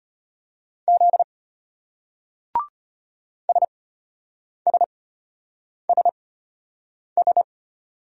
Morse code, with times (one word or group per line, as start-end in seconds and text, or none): 0.9-1.2 Z
3.5-7.4 SSSS